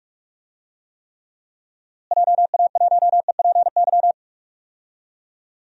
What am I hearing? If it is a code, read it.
Morse code, heard as a keyed tone, about 46 wpm, 700 Hz. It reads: JA1EPY
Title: DXCC-100 Call Signs - 6 Characters - Repeated 2x - 45wpm